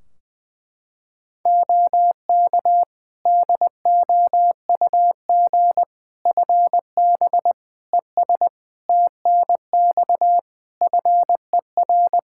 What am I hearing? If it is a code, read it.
Morse code, heard as a keyed tone, about 20 wpm, 700 Hz. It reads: OK DOUG FB ES TNX FER
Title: Practice QSO - Exchange 2 - Info - 20wpm